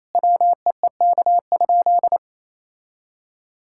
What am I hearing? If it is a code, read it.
Morse code, heard as a keyed tone, about 28 wpm, 700 Hz. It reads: WEEK?
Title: Sentences from Top 1000 Words - Review Part 5 - 28wpm